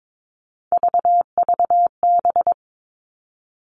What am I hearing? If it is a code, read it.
Morse code, heard as a keyed tone, about 22 wpm, 700 Hz. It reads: VVB
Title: Three Challenging Letter-Number Combinations 22wpm